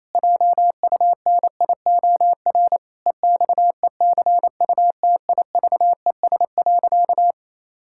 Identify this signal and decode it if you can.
Morse code, heard as a keyed tone, about 28 wpm, 700 Hz. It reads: JUNIOR EXECUTIVES.
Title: Pangram Challenge v2 - 28wpm